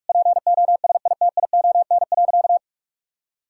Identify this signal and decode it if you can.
Morse code, heard as a keyed tone, about 46 wpm, 700 Hz. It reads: POSITION.
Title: Sentences from Top 1000 Words - Review Part 5 - 45wpm